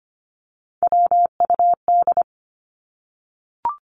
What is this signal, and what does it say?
Morse code, heard as a keyed tone, about 25 wpm, 700 Hz. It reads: WUD
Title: Abbreviations Repeated 3x - 25wpm